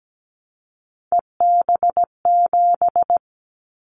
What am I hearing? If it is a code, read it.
Morse code, heard as a keyed tone, about 17 wpm, 700 Hz. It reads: EB7
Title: Callsign Prefixes for Top 25 DXCC - 17wpm